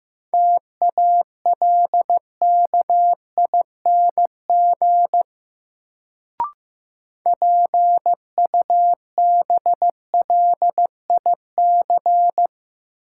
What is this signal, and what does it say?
Morse code, heard as a keyed tone, about 15 wpm, 700 Hz. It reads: TALKING PUBLIC
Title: Sets of 2 Words - New Word Drill - 15wpm